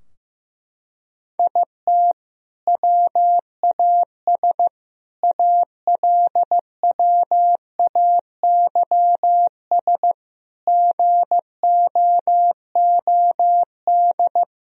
Morse code, heard as a keyed tone, about 15 wpm, 700 Hz. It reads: IT WAS ALWAYS GOOD
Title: Sentences from Top 1000 Words - Review Part 4 - 15wpm